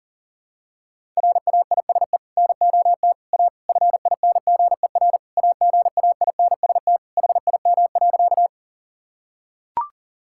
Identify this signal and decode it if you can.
Morse code, heard as a keyed tone, about 40 wpm, 700 Hz. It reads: RAISE NOT A FINGER AGAINST HIM.
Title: Sentences from Top 1600 Words 40wpm